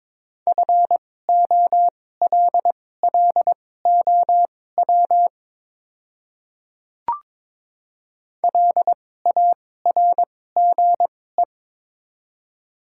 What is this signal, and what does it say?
Morse code, heard as a keyed tone, about 22 wpm, spaced as for 16 wpm, 700 Hz. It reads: FOLLOW LARGE
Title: 2x Character Spacing - Top 300 Words - 22wpm